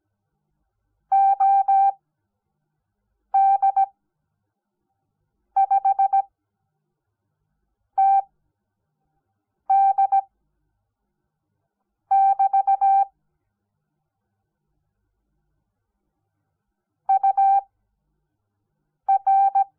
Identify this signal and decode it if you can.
Morse code, heard as a keyed tone, about 17 wpm, spaced as for 5 wpm, 780 Hz. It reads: OD5TD<BT> UR
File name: Onlinelehrgang / morsen / DJ4UF Morsen05